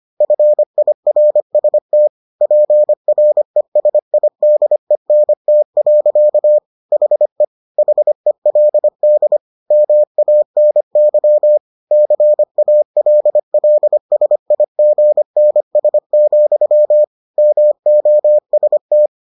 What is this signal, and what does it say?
Morse code, heard as a keyed tone, about 25 wpm, 600 Hz. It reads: FIRST PRESIDENT. HE HELD MANY CALLSIGNS, MOST